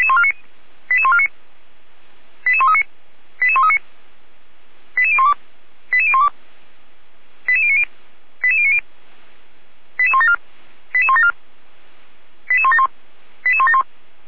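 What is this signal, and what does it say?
Five-tone selective call (ZVEI), heard as quick runs of five tones, tones of about 70 ms. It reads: 88149 88139 88139 88012 88012 88090 88090 88174 88174 88171 88171
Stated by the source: ZVEI five-tone selective call 88199 received by SDR, fire brigade Orlamünde (feuerwehr-orlamuende).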